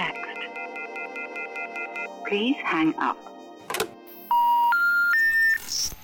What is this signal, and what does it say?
Special information tone, with speech.